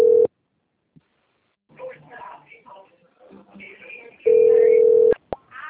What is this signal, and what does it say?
Ringback tone (North America), 440 + 480 Hz, with speech.
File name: out- - - - - .24809